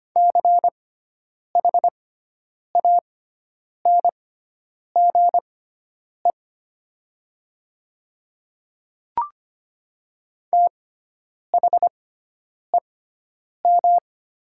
Morse code, heard as a keyed tone, about 25 wpm, spaced as for 9 wpm, 700 Hz. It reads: CHANGE THEM
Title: 6x Character Spacing - Top 200 Words - 25wpm